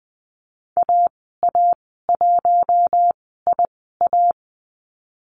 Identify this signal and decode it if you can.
Morse code, heard as a keyed tone, about 20 wpm, spaced as for 14 wpm, 700 Hz. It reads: AA1IA
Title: 042 Lesson 8g - Number 1 - Call Signs 20wpm